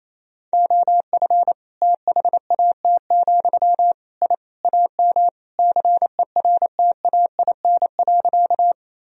Morse code, heard as a keyed tone, about 27 wpm, 700 Hz. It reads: OF THAT, I AM CERTAIN.
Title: Sentences from Top 1000 Words - Review Part 2 - 28wpm